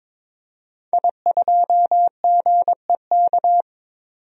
Morse code, heard as a keyed tone, about 21 wpm, 700 Hz. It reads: I2GEK
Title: DXCC-100 Call Signs - 5 Characters - 22wpm